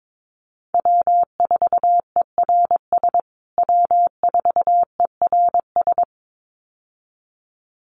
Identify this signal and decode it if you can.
Morse code, heard as a keyed tone, about 22 wpm, 700 Hz. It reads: W4ERS W4ERS
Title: USA Call Signs Repeated 2x 22wpm